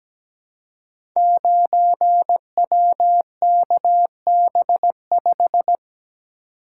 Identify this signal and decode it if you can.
Morse code, heard as a keyed tone, about 17 wpm, 700 Hz. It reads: 9WKB5
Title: Code Groups - 5 Letter:Numbers - 17wpm